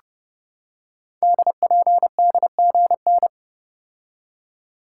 Morse code, heard as a keyed tone, about 30 wpm, 700 Hz. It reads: DPDGN
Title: Code Groups - 5 Letters - 30wpm